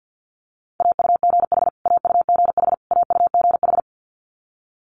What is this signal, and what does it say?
Morse code, heard as a keyed tone, about 50 wpm, 690 Hz. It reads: AUGH AUGH AUGH
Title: Multi-letter Phonemes Repeated 3x 50wpm